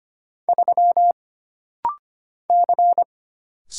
Morse code, heard as a keyed tone, about 25 wpm, 700 Hz. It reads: 3 C
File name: Single Character (Letter:Number:Punctuation:Prosign) - Rapid-Fire - 25wpm